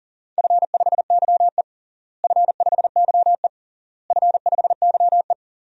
Morse code, heard as a keyed tone, about 40 wpm, 700 Hz. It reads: F5YE F5YE F5YE
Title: DXCC-100 Call Signs - 4 Characters - Repeated 3x - 40wpm